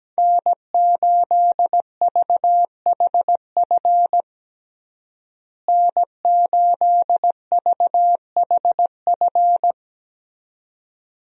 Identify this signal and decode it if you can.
Morse code, heard as a keyed tone, about 17 wpm, 700 Hz. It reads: N8VHF N8VHF
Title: USA Call Signs - 5 Characters - Repeated 2x - 17wpm